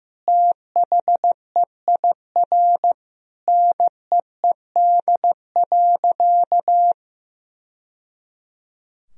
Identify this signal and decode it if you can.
Morse code, heard as a keyed tone, about 15 wpm, 700 Hz. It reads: THEIR NEED.
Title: Sentences from Top 1000 Words - Review Part 4 - 15wpm